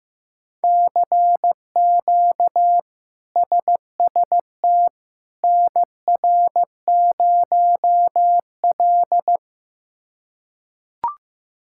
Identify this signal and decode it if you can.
Morse code, heard as a keyed tone, about 16 wpm, 700 Hz. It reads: CQ SST NR0L
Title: K1USN SST QSO Practice - 15wpm